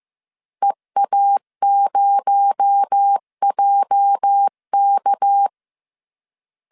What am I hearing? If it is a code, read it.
Morse code, heard as a keyed tone, about 15 wpm, 800 Hz. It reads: EA0JK
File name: CW / DET / 1 9 1500C S A 14wpm 3222